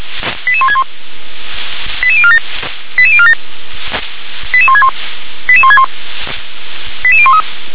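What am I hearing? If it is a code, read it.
Five-tone selective call (ZVEI), heard as quick runs of five tones, tones of about 70 ms. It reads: 88161 88047 88047 88161 88161 88013
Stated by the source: Five-tone ZVEI selective call received on an SDR.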